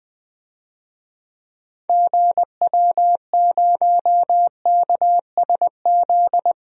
Morse code, heard as a keyed tone, about 20 wpm, 700 Hz. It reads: GW0KSZ
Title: DXCC-100 Call Signs - 6 Characters - Repeated 3x - 20wpm